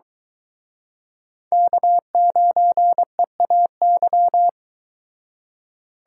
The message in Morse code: K9EAY

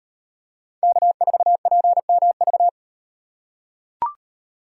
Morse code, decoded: K4PMV

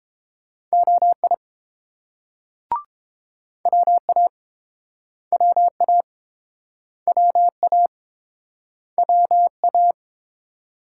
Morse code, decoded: OI WA WA WA WA

